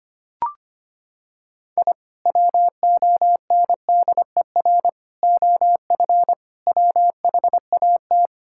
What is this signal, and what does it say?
Morse code, heard as a keyed tone, about 25 wpm, 700 Hz. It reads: I WONDER OF WHAT
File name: practice / 25wpm / Sentences with 2-Word Phrases - 25wpm